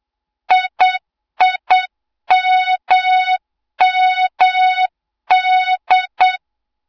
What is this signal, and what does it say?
Morse code, heard as a keyed tone, about 8 wpm, 3700 Hz, with noise. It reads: IIMMD
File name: Morze-HA2MN / hang / 20 GYAKORLAT